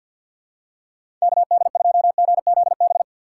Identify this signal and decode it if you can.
Morse code, heard as a keyed tone, about 53 wpm, 700 Hz. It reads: KD2GZB
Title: USA Call Signs - 6 Characters - 4x Character Spacing - 50wpm